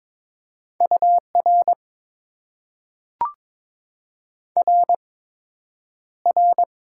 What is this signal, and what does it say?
Morse code, heard as a keyed tone, about 22 wpm, 700 Hz. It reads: U R R R
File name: Top 90 QSO Elements - 8x Character Spacing - 22wpm